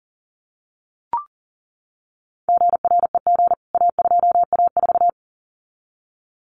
Morse code, heard as a keyed tone, about 37 wpm, 700 Hz. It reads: GREG A2A4
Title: CWT Exchange 40wpm